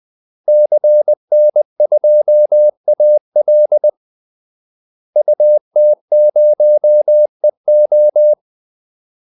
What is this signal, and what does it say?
Morse code, heard as a keyed tone, about 20 wpm, 600 Hz. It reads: CN2AL UT0EO
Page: Kaldesignaler 20 wpm